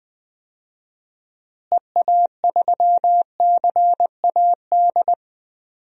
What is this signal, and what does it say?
Morse code, heard as a keyed tone, about 20 wpm, 700 Hz. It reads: EA3CAD
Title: DXCC-100 Call Signs - 6 Characters - Repeated 3x - 20wpm